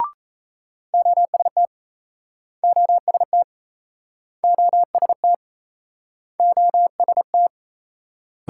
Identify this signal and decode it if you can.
Morse code, heard as a keyed tone, about 38 wpm, 700 Hz. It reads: OST OST OST OST